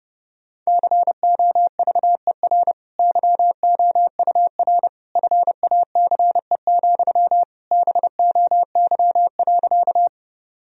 Morse code, heard as a keyed tone, about 30 wpm, 700 Hz. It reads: COVER YOUR FACE, BOY.
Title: Sentences from Top 1000 Words - Review Part 1 - 30wpm